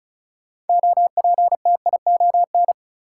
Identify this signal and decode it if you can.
Morse code, heard as a keyed tone, about 36 wpm, 700 Hz. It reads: OPTION
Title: Sentences from Top 1300 Words - New Word Drill - 35wpm